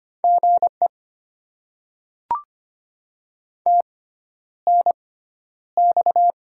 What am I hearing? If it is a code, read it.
Morse code, heard as a keyed tone, about 25 wpm, spaced as for 11 wpm, 700 Hz. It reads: GE TNX
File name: Top 20 QSO Elements - 6x Character Spacing - 25wpm